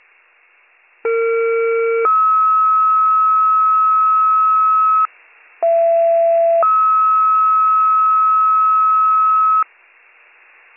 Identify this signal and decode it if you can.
Two-tone page, heard as a one-second tone, then a three-second tone, about 460 then 1290 Hz, about 670 then 1290 Hz. It is heard from 1.0 to 9.6 s.